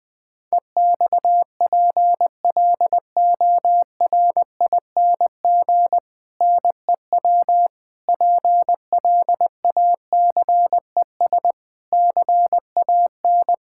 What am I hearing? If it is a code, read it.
Morse code, heard as a keyed tone, about 20 wpm, 700 Hz. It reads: EXPLORING NEW PLACES CAN
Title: Difficult Sentences with Hints - 20wpm